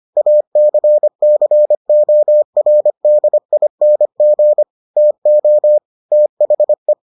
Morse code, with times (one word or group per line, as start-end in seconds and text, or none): 0.2-4.6 ACCORDING
5.0-5.8 TO
6.1-6.9 THE